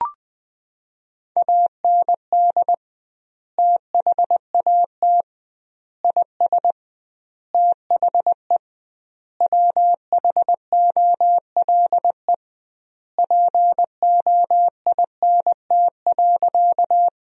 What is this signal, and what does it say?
Morse code, heard as a keyed tone, about 21 wpm, 700 Hz. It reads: AND THAT IS THE WHOLE POINT.
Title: 2x Word-Spacing - Sentences from Top 400 Words - 20wpm